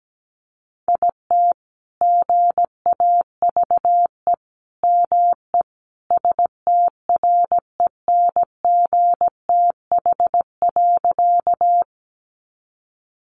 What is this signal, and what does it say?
Morse code, heard as a keyed tone, about 17 wpm, 700 Hz. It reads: IT GAVE ME STRENGTH.
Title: Sentences from Top 1300 Words 17wpm